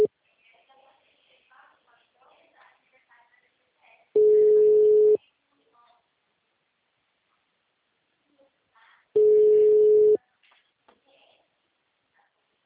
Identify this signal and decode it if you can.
Ringback tone (Europe, 425 Hz), with speech.